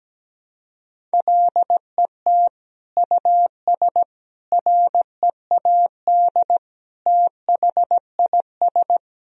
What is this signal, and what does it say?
Morse code, heard as a keyed tone, about 17 wpm, 700 Hz. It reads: LET US READ THIS
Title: Sentences from Top 1000 Words - Review Part 1 - 17wpm